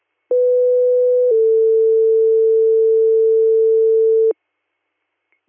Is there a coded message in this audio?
Yes: two-tone page from 0.3 to 4.3 s, heard as a one-second tone, then a three-second tone, about 500 then 450 Hz.